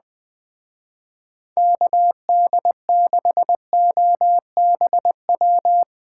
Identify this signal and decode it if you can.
Morse code, heard as a keyed tone, about 22 wpm, 700 Hz. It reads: KD6OBW
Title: Speed Racing - USA 2x3 Call Signs 20wpm